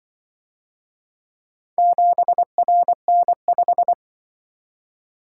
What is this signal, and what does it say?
Morse code, heard as a keyed tone, about 25 wpm, 700 Hz. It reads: 7RN5